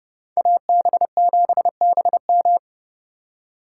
Morse code, heard as a keyed tone, about 38 wpm, 700 Hz. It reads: AB7BM